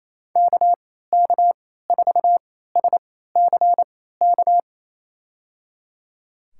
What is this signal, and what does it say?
Morse code, heard as a keyed tone, about 28 wpm, spaced as for 16 wpm, 700 Hz. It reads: KK4SCK
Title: USA Call Signs - 6 Characters - 3x Character Spacing - 28wpm